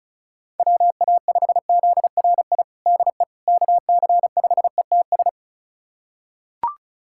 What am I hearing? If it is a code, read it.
Morse code, heard as a keyed tone, about 34 wpm, 700 Hz. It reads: WA5ZRI DE KC5ETS
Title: USA Callsign DE Callsign 35wpm